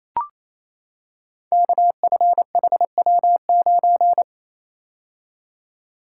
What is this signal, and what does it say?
Morse code, heard as a keyed tone, about 27 wpm, 700 Hz. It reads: KFHW9
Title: Code Groups - 5 Letter:Numbers - 28wpm